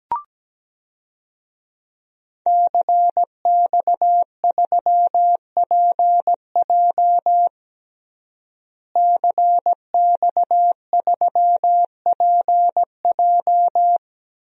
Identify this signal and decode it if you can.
Morse code, heard as a keyed tone, about 17 wpm, 700 Hz. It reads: CX3PJ CX3PJ
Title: DXCC-100 Call Signs - 5 Characters - Repeated 3x - 17wpm